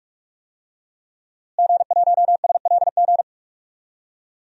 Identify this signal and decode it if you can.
Morse code, heard as a keyed tone, about 47 wpm, 700 Hz. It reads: G1SLG
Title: DXCC-100 Call Signs - 5 Characters - Repeated 2x - 45wpm